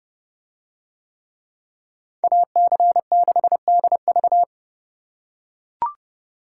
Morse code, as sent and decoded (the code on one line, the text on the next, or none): .- -.-. -.... -.. ...-
AC6DV